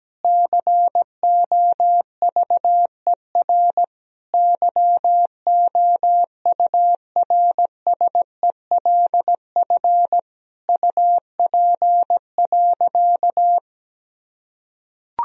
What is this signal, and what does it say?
Morse code, heard as a keyed tone, about 17 wpm, 700 Hz. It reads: COVER YOURSELF UP.
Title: Sentences from Top 1000 Words - Review Part 1 - 17wpm